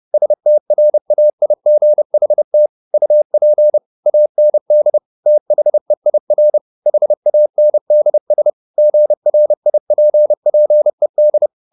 Morse code, heard as a keyed tone, about 30 wpm, 600 Hz. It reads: STRAIGHT UP AND THEIR HANDS GRIPPED
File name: the old man and the sea / the old man and the sea 1 30wpm